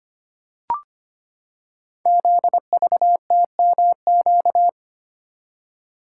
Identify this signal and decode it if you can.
Morse code, heard as a keyed tone, about 25 wpm, 700 Hz. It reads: ZVTMQ
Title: Code Groups - 5 Letters - 25wpm